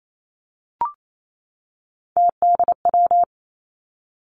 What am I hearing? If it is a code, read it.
Morse code, heard as a keyed tone, about 28 wpm, 700 Hz. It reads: TDW
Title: Three Challenging Letter-Number Combinations 28wpm